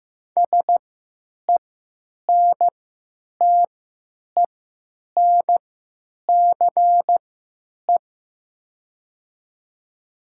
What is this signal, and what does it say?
Morse code, heard as a keyed tone, about 15 wpm, spaced as for 8 wpm, 700 Hz. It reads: SENTENCE
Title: 3x Character Spacing - Top 200 Words - 15wpm